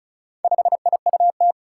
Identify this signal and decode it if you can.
Morse code, heard as a keyed tone, about 34 wpm, 700 Hz. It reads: 5IUT
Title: Four Challenging Letter-Number Combinations 35wpm